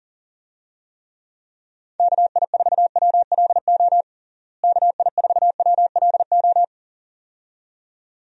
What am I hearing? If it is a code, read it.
Morse code, heard as a keyed tone, about 37 wpm, 700 Hz. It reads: KI4WLO KI4WLO
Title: USA Call Signs - 6 Characters - Repeated 2x - 40wpm